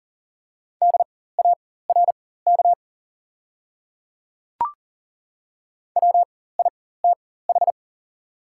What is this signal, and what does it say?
Morse code, heard as a keyed tone, about 40 wpm, spaced as for 19 wpm, 690 Hz. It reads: DARK WITH
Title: 4x Character Spacing - Top 500 Words - 40wpm